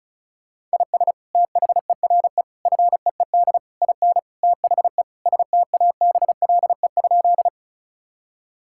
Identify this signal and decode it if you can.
Morse code, heard as a keyed tone, about 34 wpm, 700 Hz. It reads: IS THERE FEED IN THE STABLE?
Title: Sentences from Top 1600 Words 35wpm